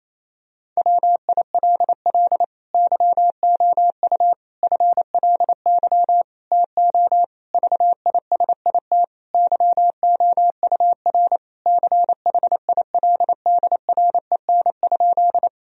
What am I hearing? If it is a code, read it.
Morse code, heard as a keyed tone, about 27 wpm, 700 Hz. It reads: WILL YOU FLY TO VISIT YOUR CHILDREN?
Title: Sentences from Top 1000 Words - Review Part 2 - 28wpm